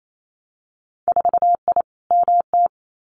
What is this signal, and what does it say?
Morse code, heard as a keyed tone, about 28 wpm, 700 Hz. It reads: 4I MT
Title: Winter Field Day Exchanges Encore 28wpm